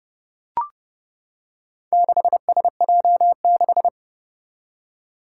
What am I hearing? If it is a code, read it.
Morse code, heard as a keyed tone, about 30 wpm, 700 Hz. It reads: 6SJ6
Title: Four Challenging Letter-Number Combinations 30wpm